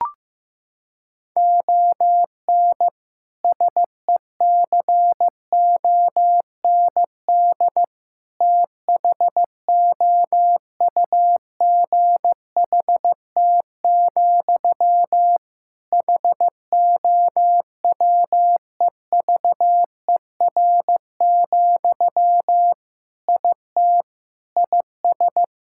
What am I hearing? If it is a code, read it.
Morse code, heard as a keyed tone, about 15 wpm, 700 Hz. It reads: ON SECOND THOUGHT, HOWEVER, IT IS
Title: Sentences from Top 1600 Words 15wpm